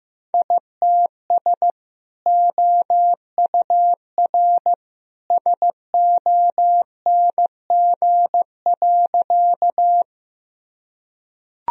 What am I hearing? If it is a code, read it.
Morse code, heard as a keyed tone, about 15 wpm, 700 Hz. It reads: ITS OUR SONG.
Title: Sentences from Top 1000 Words - Review Part 2 - 15wpm